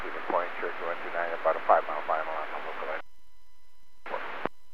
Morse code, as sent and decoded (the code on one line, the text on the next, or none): none